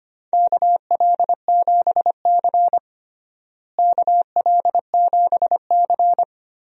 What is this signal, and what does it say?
Morse code, heard as a keyed tone, about 25 wpm, 700 Hz. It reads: KL7C KL7C
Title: DXCC-100 Call Signs - 4 Characters - Repeated 2x - 25wpm